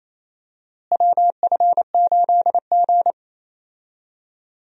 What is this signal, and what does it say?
Morse code, heard as a keyed tone, about 28 wpm, 700 Hz. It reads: WF8G
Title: USA Call Signs - 4 Characters - Repeated 3x - 28wpm